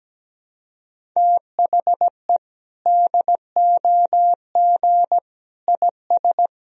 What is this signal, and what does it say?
Morse code, heard as a keyed tone, about 17 wpm, 700 Hz. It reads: THE DOG IS
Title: Sentences from Top 400 Words and Abbreviations 17wpm